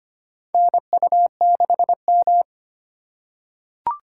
Morse code, heard as a keyed tone, about 25 wpm, 700 Hz. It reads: NU6M
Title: USA Call Signs - 2x Character Spacing - 25wpm